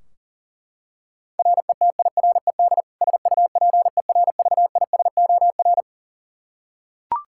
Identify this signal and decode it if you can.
Morse code, heard as a keyed tone, about 40 wpm, 700 Hz. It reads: RETIRED SUPERVISOR
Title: Practice QSO - Exchange 3 - More Info - 40wpm